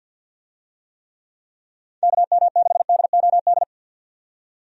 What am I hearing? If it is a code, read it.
Morse code, heard as a keyed tone, about 53 wpm, 700 Hz. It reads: KM6DOD
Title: USA Call Signs - 6 Characters - 50wpm